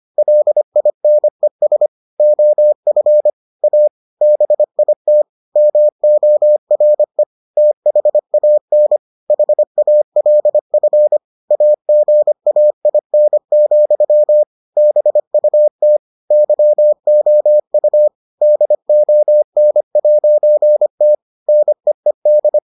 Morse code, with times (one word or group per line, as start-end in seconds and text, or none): 0.2-1.9 LINES
2.2-3.3 OF
3.6-3.9 A
4.2-5.2 BIT
5.6-7.2 MORE
7.6-9.0 THAN
9.3-11.2 HALF
11.5-14.4 AGAIN,
14.8-16.0 BUT
16.3-18.1 YOU
18.4-21.1 DON'T
21.5-22.6 NEED